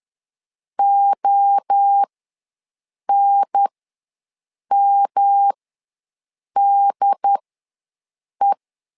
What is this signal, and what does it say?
Morse code, heard as a keyed tone, about 10 wpm, spaced as for 6 wpm, 800 Hz. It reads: ONMDE
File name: LEZIONI / 3 MWRO / 3 MWRO ws 10wpm 2310